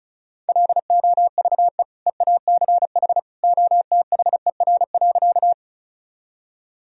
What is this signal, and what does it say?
Morse code, heard as a keyed tone, about 34 wpm, 700 Hz. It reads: LOVE EACH OTHER.
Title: Sentences from Top 1000 Words - Review Part 1 - 35wpm